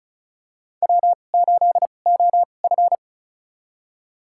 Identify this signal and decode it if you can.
Morse code, heard as a keyed tone, about 36 wpm, spaced as for 25 wpm, 700 Hz. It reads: W8OF
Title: USA Call Signs - 2x Character Spacing - 35wpm